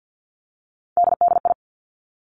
Morse code, heard as a keyed tone, about 50 wpm, 700 Hz. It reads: DDI
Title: Three Challenging Letter-Number Combinations 50wpm